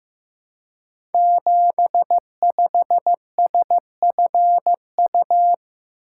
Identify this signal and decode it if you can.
Morse code, heard as a keyed tone, about 15 wpm, 700 Hz. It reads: 75SFU
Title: Code Groups - 5 Letter:Numbers - 15wpm